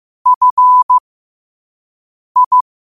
Morse code, heard as a keyed tone, about 15 wpm, spaced as for 5 wpm, 1000 Hz. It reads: FI
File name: qotdfiles / 5wpm